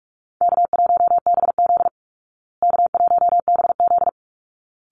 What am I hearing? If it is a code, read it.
Morse code, heard as a keyed tone, about 44 wpm, 700 Hz. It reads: K1BZ K1BZ
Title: Call Signs Repeated 3x - 45wpm